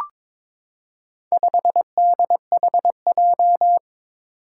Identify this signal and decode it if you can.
Morse code, heard as a keyed tone, about 22 wpm, 700 Hz. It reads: 5DHJ
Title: Four Challenging Letter-Number Combinations 22wpm